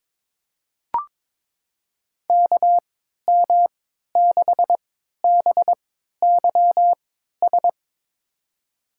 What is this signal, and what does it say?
Morse code, heard as a keyed tone, about 22 wpm, spaced as for 13 wpm, 700 Hz. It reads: KM6BYS
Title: USA Call Signs - 6 Characters - 3x Character Spacing - 22wpm